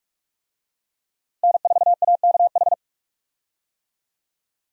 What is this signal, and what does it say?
Morse code, heard as a keyed tone, about 46 wpm, 700 Hz. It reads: N4AKH
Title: USA Call Signs - 5 Characters - Repeated 3x - 45wpm